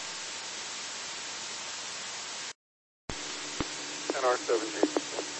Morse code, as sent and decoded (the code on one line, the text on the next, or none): none